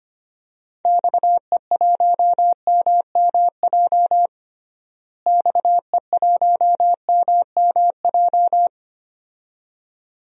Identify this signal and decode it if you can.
Morse code, heard as a keyed tone, about 25 wpm, 700 Hz. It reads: XE1MMJ XE1MMJ
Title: DXCC-100 Call Signs - 6 Characters - Repeated 2x - 25wpm